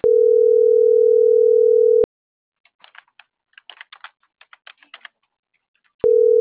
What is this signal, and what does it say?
Ringback tone (North America), 440 + 480 Hz.